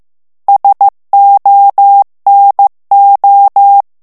Morse code, heard as a keyed tone, about 15 wpm, 800 Hz. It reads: SONO
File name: morse mp3 / ITALIEN texte clair 2 vitesse 14 mots minute